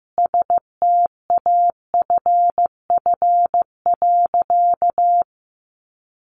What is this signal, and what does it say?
Morse code, heard as a keyed tone, about 15 wpm, 700 Hz. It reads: STAFF.
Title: Sentences from Top 1200 Words 15wpm